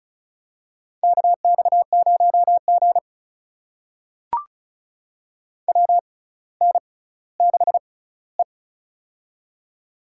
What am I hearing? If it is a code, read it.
Morse code, heard as a keyed tone, about 36 wpm, spaced as for 18 wpm, 700 Hz. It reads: KX0G WN6E
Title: USA Call Signs - 4 Characters - 6x Character Spacing - 35wpm